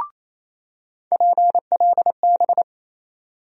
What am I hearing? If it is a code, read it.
Morse code, heard as a keyed tone, about 28 wpm, 700 Hz. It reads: PLB
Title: USA Call Sign Postfixes - 3 Letters 28wpm